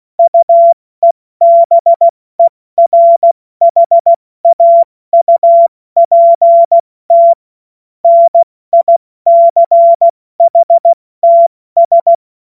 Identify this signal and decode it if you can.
Morse code, heard as a keyed tone, about 16 wpm, spaced as for 14 wpm, 680 Hz. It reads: UEBERHAUPT NICHTS
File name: Morsekurs / Witze / Witz 07. März 2023 70 F80